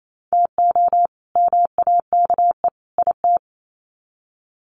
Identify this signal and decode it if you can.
Morse code, heard as a keyed tone, about 28 wpm, 700 Hz. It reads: TO MAKE IT
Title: Sets of 3 Words 28wpm